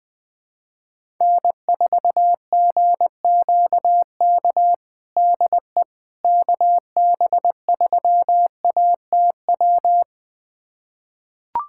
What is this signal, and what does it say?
Morse code, heard as a keyed tone, about 20 wpm, 700 Hz. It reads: N4GQK DE KB3ATW
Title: USA Callsign DE Callsign 20wpm